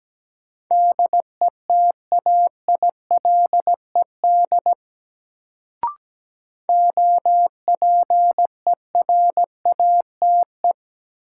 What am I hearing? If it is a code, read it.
Morse code, heard as a keyed tone, about 17 wpm, 700 Hz. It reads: DETAILED OPERATE
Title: Sentences from Top 1500 Words - New Word Drill - 17wpm